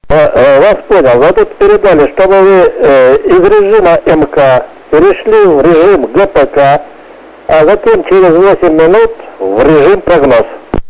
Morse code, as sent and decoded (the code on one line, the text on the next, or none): none